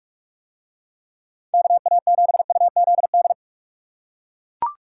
Morse code, heard as a keyed tone, about 40 wpm, 700 Hz. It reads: KA7UZD